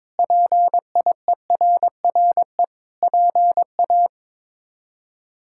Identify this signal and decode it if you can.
Morse code, heard as a keyed tone, about 22 wpm, 700 Hz. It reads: PIERRE PA
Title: K1USN SST QSO Practice - 22wpm